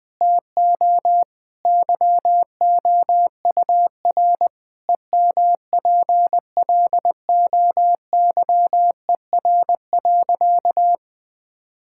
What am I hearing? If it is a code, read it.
Morse code, heard as a keyed tone, about 20 wpm, 700 Hz. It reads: TO YOUR EMPLOYER.
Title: Sentences from Top 1600 Words 20wpm